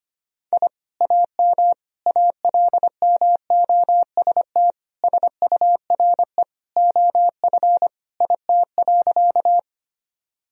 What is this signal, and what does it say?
Morse code, heard as a keyed tone, about 25 wpm, 700 Hz. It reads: I AM ALMOST SURE OF IT.
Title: Sentences from Top 1000 Words - Review Part 4 - 25wpm